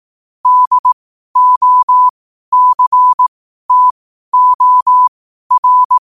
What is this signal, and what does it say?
Morse code, heard as a keyed tone, about 18 wpm, spaced as for 12 wpm, 1000 Hz. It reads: DOCTOR
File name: qotdfiles / 12wpm